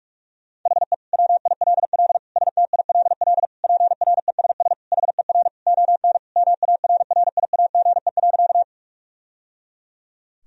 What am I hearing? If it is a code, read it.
Morse code, heard as a keyed tone, about 43 wpm, 700 Hz. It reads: HE WILL STILL PRESS HER ON MARRIAGE.
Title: Sentences from Top 1000 Words - Review Part 1 - 45wpm